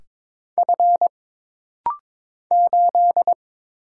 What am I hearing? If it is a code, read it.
Morse code, heard as a keyed tone, about 22 wpm, 700 Hz. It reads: F 8
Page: This rapid-fire timing is geared to push your ICR (Instant Character Recognition) skills.